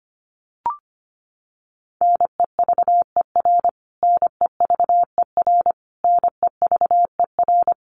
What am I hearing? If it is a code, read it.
Morse code, heard as a keyed tone, about 25 wpm, 700 Hz. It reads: NEVER NEVER NEVER
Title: Top 500 Words Repeated 25wpm